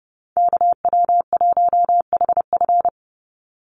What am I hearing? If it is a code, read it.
Morse code, heard as a keyed tone, about 30 wpm, 700 Hz. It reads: KW1HF
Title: USA Call Signs 30wpm